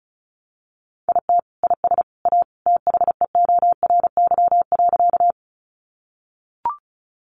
Morse code, heard as a keyed tone, about 34 wpm, 700 Hz. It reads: IT IS A THEORY.
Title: Sentences from Top 600 Words 35wpm